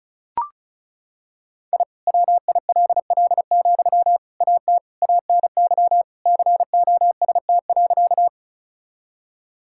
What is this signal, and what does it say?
Morse code, heard as a keyed tone, about 36 wpm, 700 Hz. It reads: I WILL, AT ANY COST.
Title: Sentences from Top 1000 Words - Review Part 5 - 35wpm